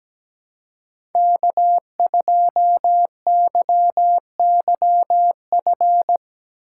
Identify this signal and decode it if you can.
Morse code, heard as a keyed tone, about 17 wpm, 700 Hz. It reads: K2YYF
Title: USA Call Signs - 8x Character Spacing - 17wpm